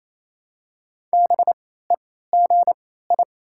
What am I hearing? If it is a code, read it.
Morse code, heard as a keyed tone, about 28 wpm, spaced as for 16 wpm, 700 Hz. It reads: BEGI
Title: 3x Character Spacing - Top 300 Words - 28wpm